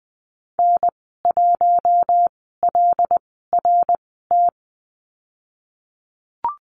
Morse code, heard as a keyed tone, about 20 wpm, spaced as for 15 wpm, 700 Hz. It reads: N1LRT
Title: 082 Lesson 14g - Number 2 - Call Signs 20wpm